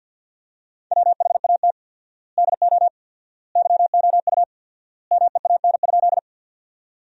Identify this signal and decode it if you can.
Morse code, heard as a keyed tone, about 52 wpm, 700 Hz. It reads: WHAT DO YOU MEAN?
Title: 4x Word-Spacing - Sentences from Top 200 Words - 50wpm